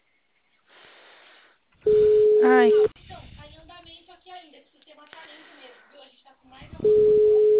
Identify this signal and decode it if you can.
Ringback tone (Europe, 425 Hz), with speech.
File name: out- -2028- - - .19112